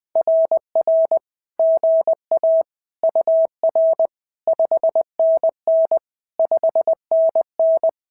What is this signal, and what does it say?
Morse code, heard as a keyed tone, about 20 wpm, 650 Hz. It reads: RR GA UR 5NN 5NN
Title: SOTA QSO Practice 20wpm